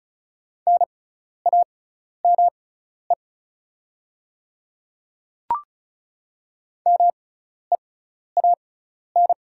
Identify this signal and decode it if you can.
Morse code, heard as a keyed tone, about 34 wpm, spaced as for 12 wpm, 700 Hz. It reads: NAME MEAN
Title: 6x Character Spacing - Top 200 Words - 35wpm